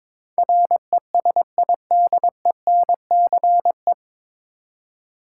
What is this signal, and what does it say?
Morse code, heard as a keyed tone, about 22 wpm, 700 Hz. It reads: RESIDENCE
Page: Morse Code Ninja practice 22wpm